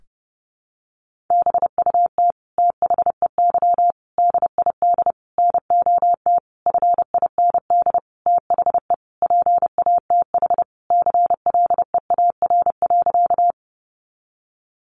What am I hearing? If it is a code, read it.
Morse code, heard as a keyed tone, about 30 wpm, 700 Hz. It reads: BUT THEY DID NOT FIND THE PATH CLEAR.
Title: Sentences from Top 1400 Words 30wpm